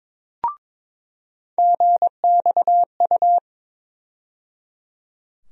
Morse code, heard as a keyed tone, about 22 wpm, 700 Hz. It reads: GXU
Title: USA Call Sign Postfixes - 3 Letters 22wpm